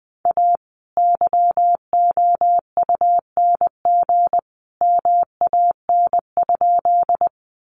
Morse code, heard as a keyed tone, about 20 wpm, 700 Hz. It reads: A YOUNG MAN?
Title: Sentences from Top 400 Words 20wpm